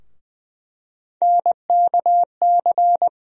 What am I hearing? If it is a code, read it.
Morse code, heard as a keyed tone, about 21 wpm, 700 Hz. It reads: NKC
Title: USA Call Sign Postfixes - 3 Letters 20wpm